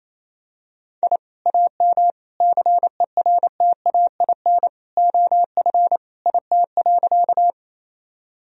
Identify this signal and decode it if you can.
Morse code, heard as a keyed tone, about 27 wpm, 700 Hz. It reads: I AM CERTAIN OF IT.
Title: Sentences from Top 1000 Words - Review Part 2 - 28wpm